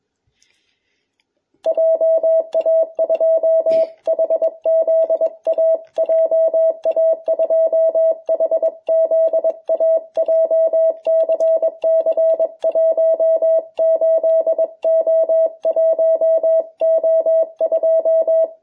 Morse code, heard as a keyed tone, about 22 wpm, 660 Hz. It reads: JA2HZA JA2HZA JCC 18O1O2